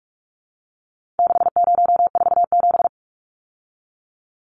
Morse code, heard as a keyed tone, about 44 wpm, 700 Hz. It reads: 6047
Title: Numbers - 4-Digit 45wpm